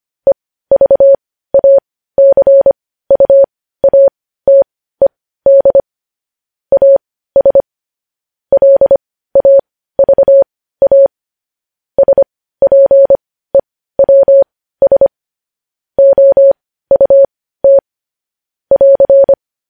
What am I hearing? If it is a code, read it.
Morse code, heard as a keyed tone, about 25 wpm, spaced as for 15 wpm, 550 Hz. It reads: EVACUATED AS LAVA SPEWS OUT <AR>
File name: podcast / News headlines-1x-25x15